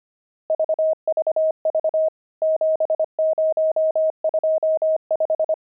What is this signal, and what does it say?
Morse code, heard as a keyed tone, about 25 wpm, 630 Hz. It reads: VVV 7025
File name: pangram-fs-25-cs-25-nums10